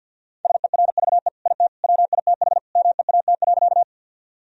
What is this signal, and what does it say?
Morse code, heard as a keyed tone, about 49 wpm, 700 Hz. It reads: SERVE IT WITH MEAT.
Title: Sentences from Top 1000 Words - Review Part 3 - 50wpm